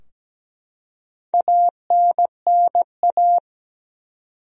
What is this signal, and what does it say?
Morse code, heard as a keyed tone, about 17 wpm, 700 Hz. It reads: ANNA
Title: USA Common Names - 17wpm